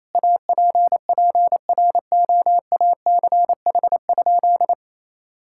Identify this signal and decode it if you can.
Morse code, heard as a keyed tone, about 28 wpm, 700 Hz. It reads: APPROACH?
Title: Sentences from Top 1000 Words - Review Part 6 - 28wpm